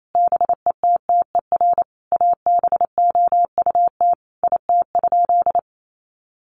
Morse code, heard as a keyed tone, about 28 wpm, 700 Hz. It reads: BETTER ABOUT IT?
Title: Sentences from Top 1500 Words 28wpm